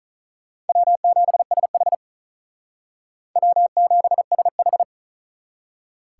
Morse code, heard as a keyed tone, about 44 wpm, 700 Hz. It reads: W7SH W7SH